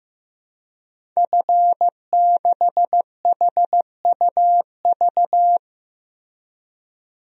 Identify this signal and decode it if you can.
Morse code, heard as a keyed tone, about 15 wpm, 700 Hz. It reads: F6HUV
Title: DXCC-100 Call Signs - 5 Characters - 15wpm